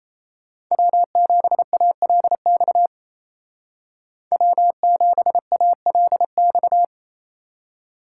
Morse code, heard as a keyed tone, about 33 wpm, 700 Hz. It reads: W7ALX W7ALX
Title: Speed Racing - USA Call Signs 28wpm